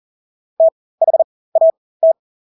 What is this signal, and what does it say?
Morse code, heard as a keyed tone, about 40 wpm, spaced as for 20 wpm, 650 Hz. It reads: THAT
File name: Engelske ord 40-20 wpm